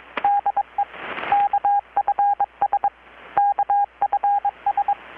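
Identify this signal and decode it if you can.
Morse code, heard as a keyed tone, about 22 wpm, 810 Hz, with radio noise. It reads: DE KFS KFS
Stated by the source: Morse from KPH's Night of Nights XX maritime radiotelegraphy event, received in Chiapas, Mexico about 3,800 km away on a Kenwood R-600 with a grounded 10 m long-wire antenna.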